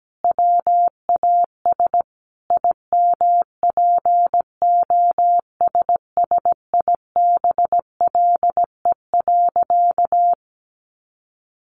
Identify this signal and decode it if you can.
Morse code, heard as a keyed tone, about 17 wpm, 700 Hz. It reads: WAS IMPOSSIBLE.
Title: Sentences from Top 1100 Words 17wpm